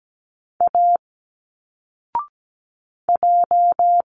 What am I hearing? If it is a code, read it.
Morse code, heard as a keyed tone, about 17 wpm, 700 Hz. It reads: A J
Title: Single Letters 17wpm